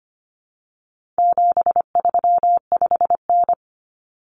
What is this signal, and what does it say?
Morse code, heard as a keyed tone, about 25 wpm, 700 Hz. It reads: 735N
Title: Common Cut Numbers - 4-Digit 25wpm